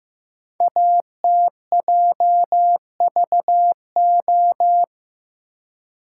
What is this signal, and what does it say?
Morse code, heard as a keyed tone, about 15 wpm, 700 Hz. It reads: ATJVO
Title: Code Groups - 5 Letters - 15wpm